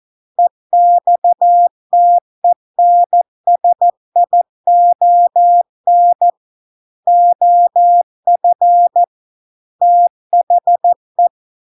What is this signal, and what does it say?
Morse code, heard as a keyed tone, about 14 wpm, 700 Hz. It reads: EXTENSION OF THE